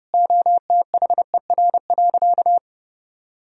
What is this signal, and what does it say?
Morse code, heard as a keyed tone, about 30 wpm, 700 Hz. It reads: OTHER.
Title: Sentences from Top 1000 Words - Review Part 1 - 30wpm